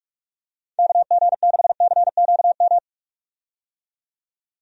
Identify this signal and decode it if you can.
Morse code, heard as a keyed tone, about 44 wpm, 700 Hz. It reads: KG6CQM
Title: USA Call Signs - 6 Characters - 45wpm